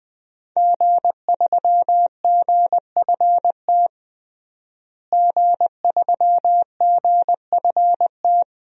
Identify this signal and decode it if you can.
Morse code, heard as a keyed tone, about 20 wpm, 700 Hz. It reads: G3GFT G3GFT
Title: DXCC-100 Call Signs - 5 Characters - Repeated 2x - 20wpm